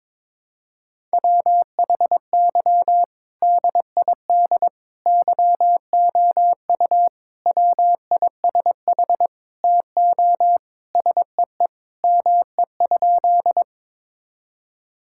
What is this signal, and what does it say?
Morse code, heard as a keyed tone, about 22 wpm, 700 Hz. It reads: WHY DID YOU WISH TO SEE ME?
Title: Sentences from Top 1600 Words 22wpm